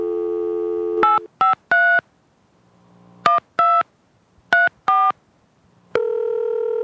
DTMF, heard as telephone touch tones, tones of about 170 ms. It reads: 853 12 34
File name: Telephone sequence #2 (touchtone phone)